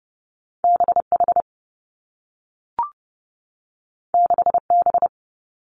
Morse code, heard as a keyed tone, about 30 wpm, 700 Hz. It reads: B H 6 B
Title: Two Challenging Letter-Number Combinations 30wpm